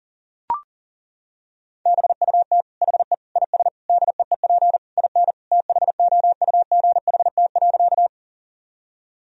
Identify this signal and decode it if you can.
Morse code, heard as a keyed tone, about 37 wpm, 700 Hz. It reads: BUT HE IS DEEP IN THOUGHT.
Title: Sentences from Top 1000 Words - Review Part 3 - 40wpm